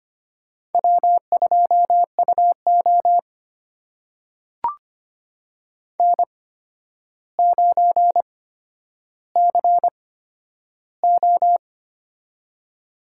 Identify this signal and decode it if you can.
Morse code, heard as a keyed tone, about 25 wpm, spaced as for 11 wpm, 700 Hz. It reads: W2UO N9CO
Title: USA Call Signs - 4 Characters - 8x Character Spacing - 25wpm